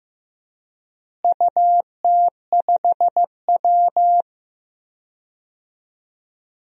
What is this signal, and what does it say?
Morse code, heard as a keyed tone, about 15 wpm, 700 Hz. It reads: UT5W
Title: DXCC-100 Call Signs - 4 Characters - Repeated 2x - 15wpm